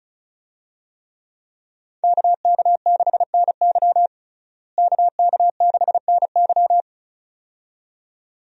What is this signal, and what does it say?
Morse code, heard as a keyed tone, about 34 wpm, 700 Hz. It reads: KK6NY KK6NY
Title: USA Call Signs - 5 Characters - Repeated 2x - 35wpm